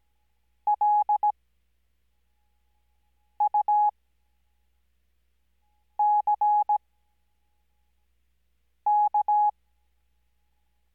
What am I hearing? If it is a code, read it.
Morse code, heard as a keyed tone, about 17 wpm, spaced as for 4 wpm, 840 Hz. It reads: LUCK
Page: Farnsworth Spacing: 15WPM Character speed Spaced at 5WPM.